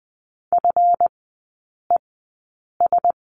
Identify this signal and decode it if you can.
Morse code, heard as a keyed tone, about 20 wpm, spaced as for 8 wpm, 700 Hz. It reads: FES
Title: 138 Lesson 23e - Letter f - Three Characters 20wpm